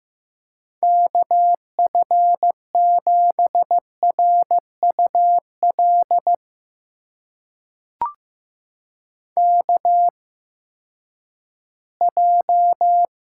Morse code, heard as a keyed tone, about 15 wpm, spaced as for 10 wpm, 700 Hz. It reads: KF7RUL KJ
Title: USA Call Signs - 6 Characters - 8x Character Spacing - 15wpm